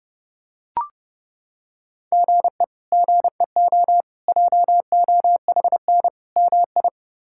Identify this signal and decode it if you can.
Morse code, heard as a keyed tone, about 30 wpm, 700 Hz. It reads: GE GEO JOHN MI
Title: K1USN SST QSO Practice - 30wpm